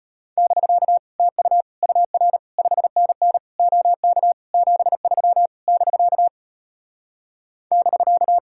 Morse code, heard as a keyed tone, about 33 wpm, 700 Hz. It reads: <BK> TU UR 5NN OK 73 <BK> <BK>